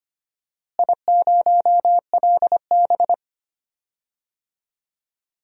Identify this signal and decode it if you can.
Morse code, heard as a keyed tone, about 25 wpm, 700 Hz. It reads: I0LB
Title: DXCC-100 Call Signs - 4 Characters - Repeated 3x - 25wpm